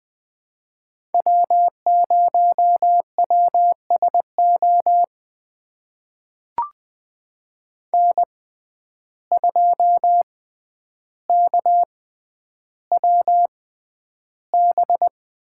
Morse code, heard as a keyed tone, about 20 wpm, spaced as for 10 wpm, 700 Hz. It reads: W0WSO N2KWB
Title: USA Call Signs - 5 Characters - 6x Character Spacing - 20wpm